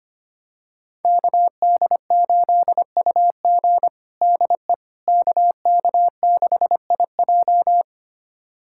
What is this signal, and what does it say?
Morse code, heard as a keyed tone, about 25 wpm, 700 Hz. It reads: KD8UG DE KK6IJ
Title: USA Callsign DE Callsign - 5 Characters - 25wpm